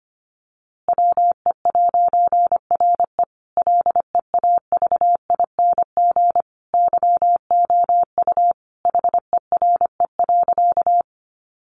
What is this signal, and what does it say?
Morse code, heard as a keyed tone, about 25 wpm, 700 Hz. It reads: WE'RE LEAVING YOU HERE.
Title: A Sound Of Thunder 25wpm